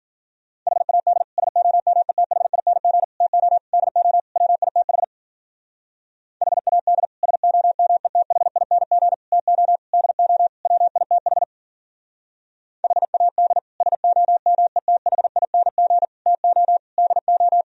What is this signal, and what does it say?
Morse code, heard as a keyed tone, about 52 wpm, 700 Hz. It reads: HAD SOMETHING TO DO WITH HAD SOMETHING TO DO WITH HAD SOMETHING TO DO